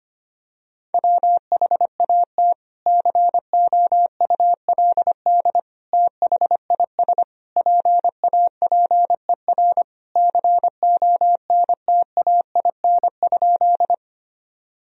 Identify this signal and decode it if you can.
Morse code, heard as a keyed tone, about 25 wpm, 700 Hz. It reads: WHAT COULD THIS PAPER CONTAIN?
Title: Sentences from Top 1000 Words - Review Part 2 - 25wpm